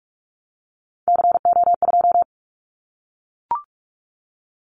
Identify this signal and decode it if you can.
Morse code, heard as a keyed tone, about 44 wpm, 700 Hz. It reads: CO2
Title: Callsign Prefixes for Top 100 DXCC - 45wpm